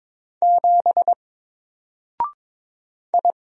Morse code, heard as a keyed tone, about 22 wpm, 700 Hz. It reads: 7 I
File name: Single-Letter-Number-22wpm